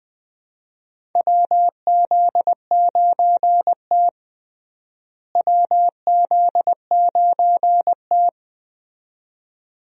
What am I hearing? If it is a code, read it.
Morse code, heard as a keyed tone, about 21 wpm, 700 Hz. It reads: WZ9T WZ9T
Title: USA Call Signs - 4 Characters - Repeated 2x - 20wpm